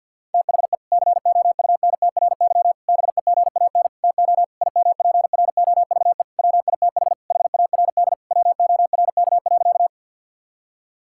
Morse code, heard as a keyed tone, about 50 wpm, 700 Hz. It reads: THE COUNTRY BEGAN TO IMPROVE WITH HARD WORK.
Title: Sentences from Top 1000 Words - Review Part 5 - 50wpm